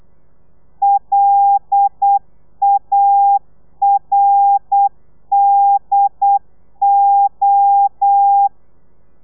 Morse code, heard as a keyed tone, about 8 wpm, 800 Hz. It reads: LARDO
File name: Lez11